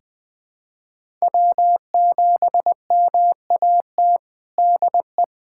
Morse code, heard as a keyed tone, about 21 wpm, 700 Hz. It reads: W7MAT DE
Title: USA Callsign DE Callsign - 5 Characters - 20wpm